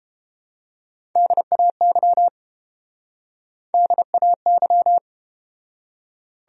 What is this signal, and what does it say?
Morse code, heard as a keyed tone, about 30 wpm, 700 Hz. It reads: DAY DAY